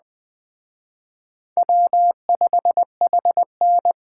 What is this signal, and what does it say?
Morse code, heard as a keyed tone, about 19 wpm, 700 Hz. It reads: W5HN